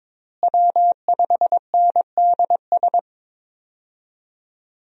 Morse code, heard as a keyed tone, about 22 wpm, 700 Hz. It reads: W5NDS
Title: USA Call Signs - 5 Characters - Repeated 2x - 22wpm